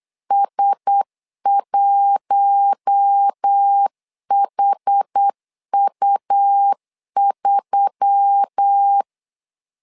Morse code, heard as a keyed tone, about 8 wpm, 800 Hz. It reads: S1HU3
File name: LEZIONI / 5 LBVC / 5 L 8wpm 2831